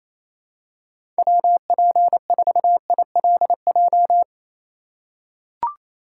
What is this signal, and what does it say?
Morse code, heard as a keyed tone, about 27 wpm, 700 Hz. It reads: WP4ILJ
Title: USA Call Signs - 6 Characters - 4x Character Spacing - 28wpm